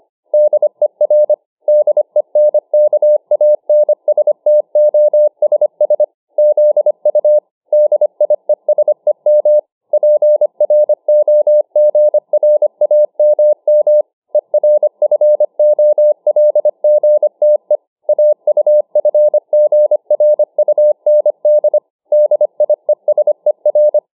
Morse code, heard as a keyed tone, about 25 wpm, 600 Hz. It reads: DER DENKANSTOSS ZU DIESEM PROGRAMM ERFOLGTE AUFGRUND DIESER